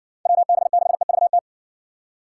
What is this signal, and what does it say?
Morse code, heard as a keyed tone, about 60 wpm, 700 Hz. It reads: WB6EVT